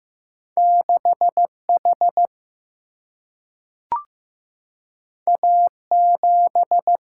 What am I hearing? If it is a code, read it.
Morse code, heard as a keyed tone, about 15 wpm, 700 Hz. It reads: 6 H A 7
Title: Single-Letter Single-Number 15wpm